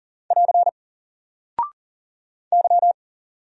Morse code, heard as a keyed tone, about 39 wpm, 700 Hz. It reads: <AR> Y
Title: Single Character (Letter:Number:Punctuation:Prosign) Alt 40wpm